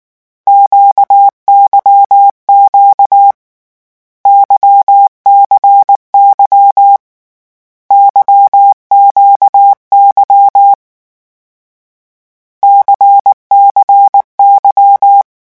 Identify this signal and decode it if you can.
Morse code, heard as a keyed tone, about 19 wpm, 800 Hz. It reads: QYQ YCY YQY CCY
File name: morse mp3 / lecon2-vitesse 18